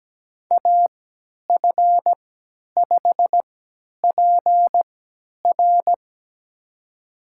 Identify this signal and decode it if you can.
Morse code, heard as a keyed tone, about 17 wpm, spaced as for 10 wpm, 700 Hz. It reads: AF5PR
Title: USA Call Signs - 3x Character Spacing - 17wpm